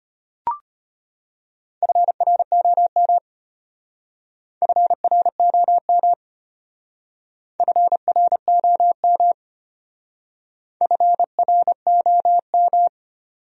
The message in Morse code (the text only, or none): FROM FROM FROM FROM